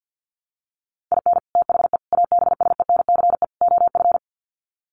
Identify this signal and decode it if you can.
Morse code, heard as a keyed tone, about 48 wpm, 700 Hz. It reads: IN THE ABSENCE OF
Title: Sets of 4 Words 50wpm